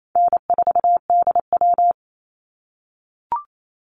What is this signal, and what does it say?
Morse code, heard as a keyed tone, about 28 wpm, 700 Hz. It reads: N4DW
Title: Call Signs - 28wpm